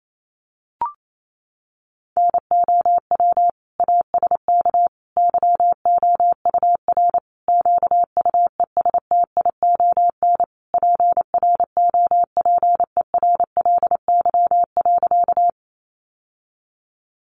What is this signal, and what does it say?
Morse code, heard as a keyed tone, about 28 wpm, 700 Hz. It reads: NOW ASK YOUR QUESTION PROPERLY.
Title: Sentences from Top 1100 Words 28wpm